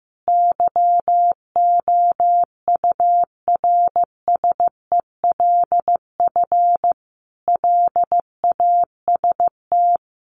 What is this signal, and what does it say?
Morse code, heard as a keyed tone, about 14 wpm, 700 Hz. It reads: YOURSELF LAST
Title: Sentences from Top 600 Words 15wpm